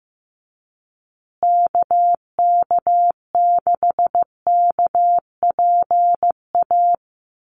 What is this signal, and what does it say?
Morse code, heard as a keyed tone, about 15 wpm, 700 Hz. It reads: KK6KPA
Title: USA Call Signs Repeated 3x 15wpm